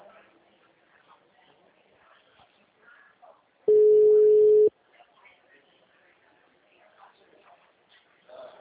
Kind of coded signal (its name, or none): telephone call progress tone